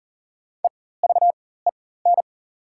Morse code, heard as a keyed tone, about 38 wpm, spaced as for 19 wpm, 700 Hz. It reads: EVEN
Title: 4x Character Spacing - Top 200 Words - 40wpm